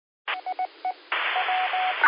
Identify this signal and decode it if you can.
Morse code, heard as a keyed tone, about 19 wpm, 740 Hz, with radio noise. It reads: IE W
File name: qsorec SimplexLogic 10-00-49